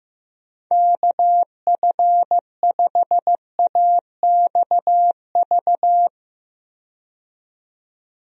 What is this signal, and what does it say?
Morse code, heard as a keyed tone, about 16 wpm, 700 Hz. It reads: KF5AXV
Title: USA Call Signs - 6 Characters - 15wpm